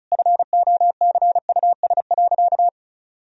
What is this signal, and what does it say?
Morse code, heard as a keyed tone, about 36 wpm, 700 Hz. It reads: FOCUS.
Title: Sentences from Top 1000 Words - Review Part 4 - 35wpm